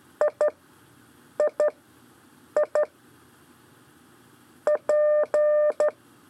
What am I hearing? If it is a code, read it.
Morse code, heard as a keyed tone, about 12 wpm, 600 Hz. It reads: I I I P
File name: 4 LEZIONE CW 1